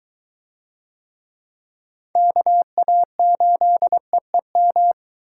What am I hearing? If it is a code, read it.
Morse code, heard as a keyed tone, about 21 wpm, 700 Hz. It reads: KA8EEM